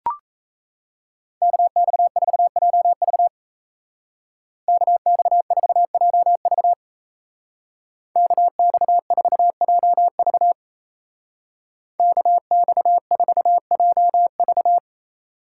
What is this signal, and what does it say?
Morse code, heard as a keyed tone, about 35 wpm, 700 Hz. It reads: KX4JV KX4JV KX4JV KX4JV